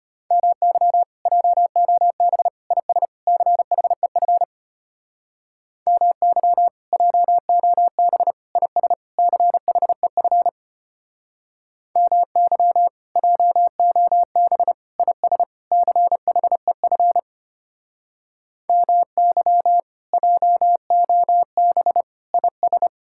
Morse code, heard as a keyed tone, about 33 wpm, 700 Hz. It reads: MY JOB IS CHEF MY JOB IS CHEF MY JOB IS CHEF MY JOB IS